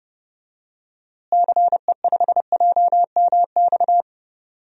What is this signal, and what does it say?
Morse code, heard as a keyed tone, about 32 wpm, 700 Hz. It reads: CE5JMX
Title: DXCC-100 Call Signs - 6 Characters - Repeated 2x - 30wpm